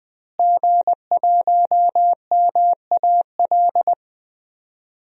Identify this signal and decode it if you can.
Morse code, heard as a keyed tone, about 20 wpm, 700 Hz. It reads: G1MAL
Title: DXCC-100 Call Signs - 5 Characters - Repeated 3x - 20wpm